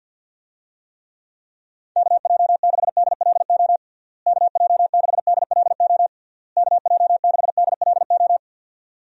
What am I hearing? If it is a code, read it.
Morse code, heard as a keyed tone, about 50 wpm, 700 Hz. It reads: KJ6DLO KJ6DLO KJ6DLO
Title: USA Call Signs - 6 Characters - Repeated 3x - 50wpm